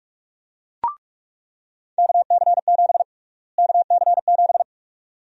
Morse code, heard as a keyed tone, about 46 wpm, 700 Hz. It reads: KC7 KC7
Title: USA Call Sign Prefixes - Repeated 2x - 45wpm